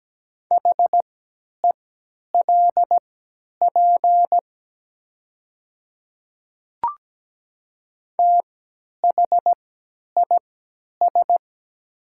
Morse code, heard as a keyed tone, about 17 wpm, spaced as for 10 wpm, 700 Hz. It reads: HELP THIS
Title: 3x Character Spacing - Top 300 Words - 17wpm